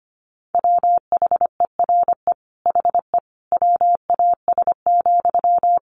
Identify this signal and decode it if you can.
Morse code, heard as a keyed tone, about 25 wpm, 700 Hz. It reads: WHERE HE WAS,
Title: Sentences from Top 400 Words 25wpm